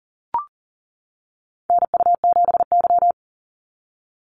Morse code, heard as a keyed tone, about 40 wpm, 700 Hz. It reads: NU7Y
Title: NA SOTA Activator Callsigns 40wpm